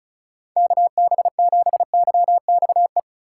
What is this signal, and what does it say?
Morse code, heard as a keyed tone, about 36 wpm, 700 Hz. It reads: KB7YXE
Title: USA Call Signs - 6 Characters - 4x Character Spacing - 35wpm